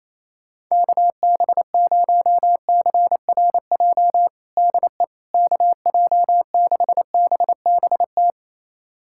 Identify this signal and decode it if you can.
Morse code, heard as a keyed tone, about 28 wpm, 700 Hz. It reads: KB0CRJ DE KJ6BBT
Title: USA Callsign DE Callsign - 6 Characters - 28wpm